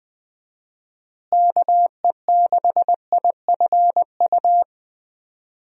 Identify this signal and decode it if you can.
Morse code, heard as a keyed tone, about 20 wpm, 700 Hz. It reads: KE6IFU
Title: USA Call Signs - 6 Characters - Repeated 2x - 20wpm